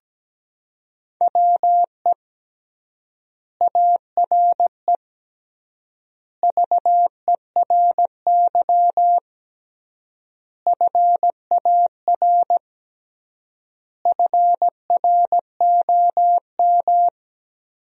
Morse code, heard as a keyed tone, about 17 wpm, 700 Hz. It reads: WE ARE VERY FAR FROM
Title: 3x Word-Spacing - Sentences from Top 300 Words - 17wpm